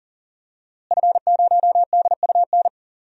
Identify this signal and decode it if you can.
Morse code, heard as a keyed tone, about 40 wpm, 700 Hz. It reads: F0DUN
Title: DXCC-100 Call Signs - 5 Characters - Repeated 2x - 40wpm